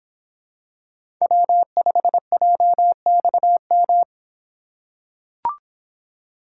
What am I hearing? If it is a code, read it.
Morse code, heard as a keyed tone, about 26 wpm, 700 Hz. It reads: W5JXM